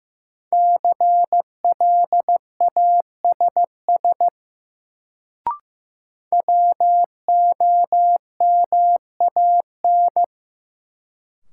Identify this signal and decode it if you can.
Morse code, heard as a keyed tone, about 15 wpm, 700 Hz. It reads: CLASS WOMAN
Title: Sentences from Top 1000 Words - Review Part 2 - New Word Drill - 15wpm